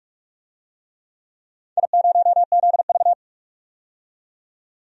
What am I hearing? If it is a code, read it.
Morse code, heard as a keyed tone, about 46 wpm, 700 Hz. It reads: I0ZV
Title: DXCC-100 Call Signs - 4 Characters - 45wpm